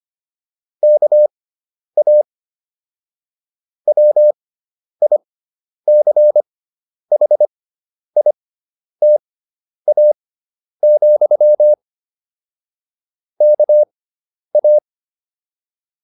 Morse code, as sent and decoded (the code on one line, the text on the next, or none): -.- .- / .-- .. -.-. .... .. - .- --..-- / -.- .-
KA WICHITA, KA